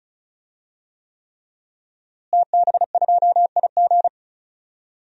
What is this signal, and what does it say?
Morse code, heard as a keyed tone, about 36 wpm, 700 Hz. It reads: TB2IG
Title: DXCC-100 Call Signs - 5 Characters - Repeated 3x - 35wpm